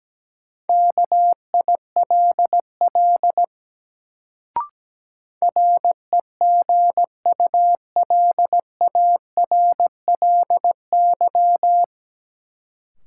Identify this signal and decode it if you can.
Morse code, heard as a keyed tone, about 17 wpm, 700 Hz. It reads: KILL REGULARLY
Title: Sentences from Top 1400 Words - New Word Drill - 17wpm